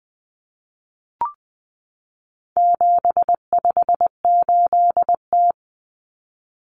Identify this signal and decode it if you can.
Morse code, heard as a keyed tone, about 20 wpm, 700 Hz. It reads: 758T
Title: Common Cut Numbers - 4-Digit 20wpm